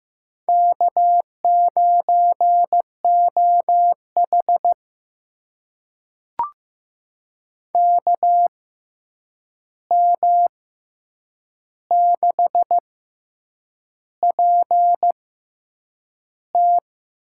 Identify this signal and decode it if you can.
Morse code, heard as a keyed tone, about 15 wpm, spaced as for 7 wpm, 700 Hz. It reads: K9OH KM6PT
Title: USA Call Signs - 6x Character Spacing - 15wpm